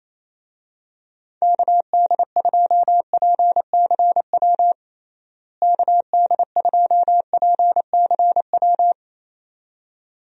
Morse code, heard as a keyed tone, about 28 wpm, 700 Hz. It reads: KD2PCW KD2PCW
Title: USA Call Signs - 6 Characters - Repeated 2x - 28wpm